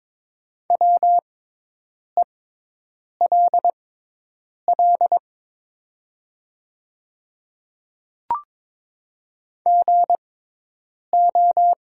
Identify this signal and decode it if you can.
Morse code, heard as a keyed tone, about 23 wpm, spaced as for 8 wpm, 700 Hz. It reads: WELL GO
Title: 6x Character Spacing - Top 200 Words - 22wpm